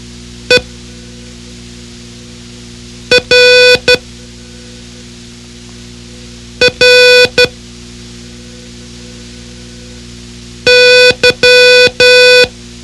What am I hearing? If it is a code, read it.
Morse code, heard as a keyed tone, about 12 wpm, 500 Hz. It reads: E R R Y
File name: ham radio / CodeQuick / Code Quick 04